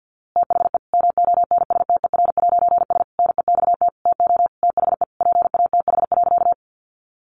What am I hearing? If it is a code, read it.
Morse code, heard as a keyed tone, about 50 wpm, 700 Hz. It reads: THE MONSTER'S NEXT TO THE PATH.
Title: A Sound Of Thunder 50wpm